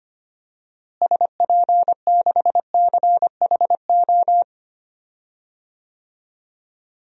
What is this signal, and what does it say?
Morse code, heard as a keyed tone, about 25 wpm, 700 Hz. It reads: SP6CHO
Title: DXCC-100 Call Signs - 6 Characters - Repeated 3x - 25wpm